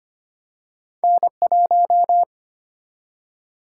Morse code, heard as a keyed tone, about 25 wpm, 700 Hz. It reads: N1
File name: Two Letter-Number Combinations - 25wpm